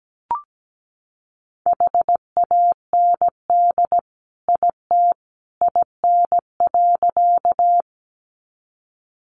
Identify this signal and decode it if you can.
Morse code, heard as a keyed tone, about 17 wpm, 700 Hz. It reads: HAND IT IN.
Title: Sentences from Top 200 Words 17wpm